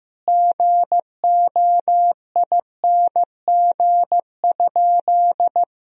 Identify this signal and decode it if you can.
Morse code, heard as a keyed tone, about 15 wpm, 700 Hz. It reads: GOING?
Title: Sentences from Top 1000 Words - Review Part 2 - 15wpm